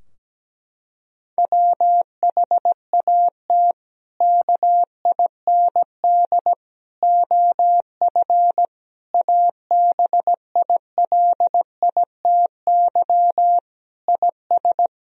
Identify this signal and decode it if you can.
Morse code, heard as a keyed tone, about 17 wpm, 700 Hz. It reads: WHAT KIND OF ABILITY IS
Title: Sentences from Top 1000 Words - Review Part 3 - 17wpm